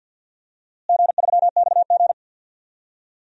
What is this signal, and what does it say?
Morse code, heard as a keyed tone, about 47 wpm, 700 Hz. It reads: G3XG
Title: DXCC-100 Call Signs - 4 Characters - Repeated 3x - 50wpm